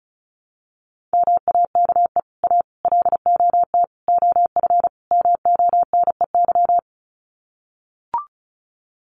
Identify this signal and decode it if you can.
Morse code, heard as a keyed tone, about 34 wpm, 700 Hz. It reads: MAKE A LOT OF MONEY
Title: Sets of 5 Words Encore2 35wpm